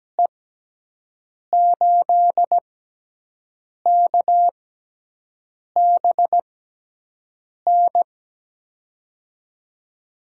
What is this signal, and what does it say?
Morse code, heard as a keyed tone, about 17 wpm, spaced as for 6 wpm, 700 Hz. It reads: E8KBN
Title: USA Call Signs - 6 Characters - 6x Character Spacing - 17wpm